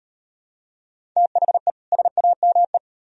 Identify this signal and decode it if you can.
Morse code, heard as a keyed tone, about 38 wpm, 700 Hz. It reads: THE SAME